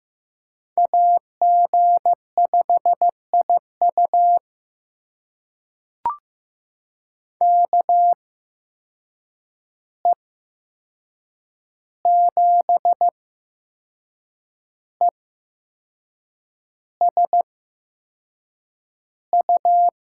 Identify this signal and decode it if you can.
Morse code, heard as a keyed tone, about 15 wpm, spaced as for 6 wpm, 700 Hz. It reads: AG5IU KE7ESU
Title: USA Call Signs - 8x Character Spacing - 15wpm